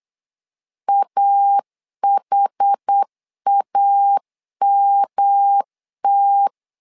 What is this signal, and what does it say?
Morse code, heard as a keyed tone, about 8 wpm, 800 Hz. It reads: AHAMT
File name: LEZIONI / 4 Q 8wpm 2732